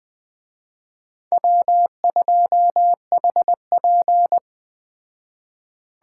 Morse code, heard as a keyed tone, about 18 wpm, 700 Hz. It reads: W2HP
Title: Speed Racing - USA Call Signs 15wpm